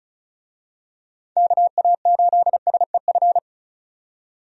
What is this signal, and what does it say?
Morse code, heard as a keyed tone, about 36 wpm, 700 Hz. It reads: KA8SEF
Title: USA Call Signs - 6 Characters - 35wpm